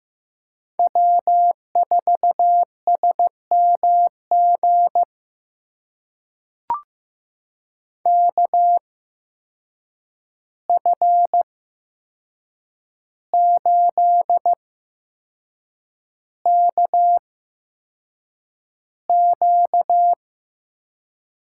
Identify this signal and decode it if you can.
Morse code, heard as a keyed tone, about 15 wpm, spaced as for 6 wpm, 700 Hz. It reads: W4SMG KF8KQ
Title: USA Call Signs - 5 Characters - 8x Character Spacing - 15wpm